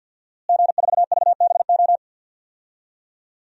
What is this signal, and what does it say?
Morse code, heard as a keyed tone, about 50 wpm, 700 Hz. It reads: G4VBO
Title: DXCC-100 Call Signs - 5 Characters - Repeated 3x - 50wpm